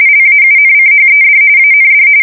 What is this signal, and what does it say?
RTTY, about 45.45 baud. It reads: LAUNCHED AND
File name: fsk170 / USS-New-Jersey